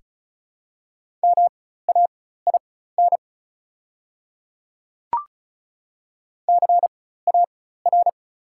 Morse code, heard as a keyed tone, about 36 wpm, spaced as for 16 wpm, 690 Hz. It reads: MAIN CAR
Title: 4x Character Spacing - Top 300 Words - 35wpm